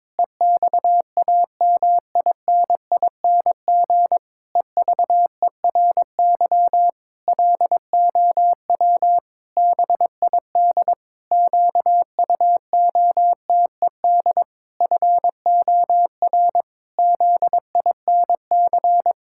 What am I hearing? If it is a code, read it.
Morse code, heard as a keyed tone, about 22 wpm, 700 Hz. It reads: EXAMINING EVERY LOW BID QUOTED FOR ZINC
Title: Pangram Challenge v2 - 22wpm